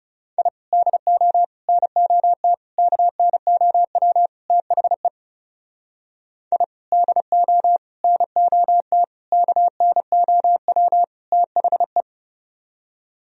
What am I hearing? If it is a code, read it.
Morse code, heard as a keyed tone, about 37 wpm, 700 Hz. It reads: I DO NOT KNOW THE I DO NOT KNOW THE
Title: Speed-Racing - Sets of 5 Words 30wpm